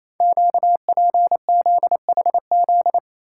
Morse code, heard as a keyed tone, about 28 wpm, 700 Hz. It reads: QPZHZ
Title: Code Groups - 5 Letters - 28wpm